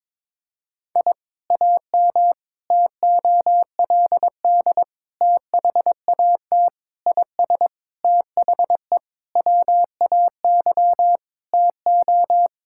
Morse code, heard as a keyed tone, about 22 wpm, 700 Hz. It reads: I AM TOLD THAT IS THE WAY TO
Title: Sentences from Top 1000 Words - Review Part 5 - 22wpm